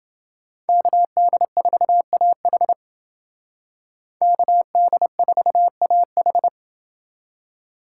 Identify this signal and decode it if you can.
Morse code, heard as a keyed tone, about 29 wpm, 700 Hz. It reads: KD4AH KD4AH